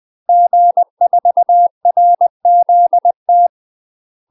Morse code, heard as a keyed tone, about 20 wpm, 700 Hz. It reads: G4RZT
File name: UK MASTER QSOs TRAINING / 20WPM / UK MASTER QSOs 0044